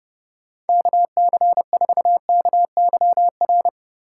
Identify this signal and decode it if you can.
Morse code, heard as a keyed tone, about 36 wpm, 700 Hz. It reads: KC4KYR
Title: Speed Racing - USA 2x3 Call Signs 30wpm